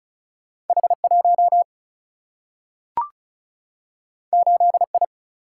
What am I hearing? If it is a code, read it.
Morse code, heard as a keyed tone, about 36 wpm, 700 Hz. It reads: H 1 8 I
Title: Single-Letter Single-Number 35wpm